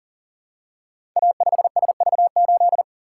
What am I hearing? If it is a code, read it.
Morse code, heard as a keyed tone, about 39 wpm, 700 Hz. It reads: A5SV8
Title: Code Groups - 5 Letter:Numbers - 40wpm